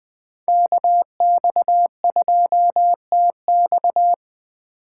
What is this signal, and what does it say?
Morse code, heard as a keyed tone, about 21 wpm, 700 Hz. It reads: KX2TX
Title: Speed Racing - USA 2x2 Call Signs 17wpm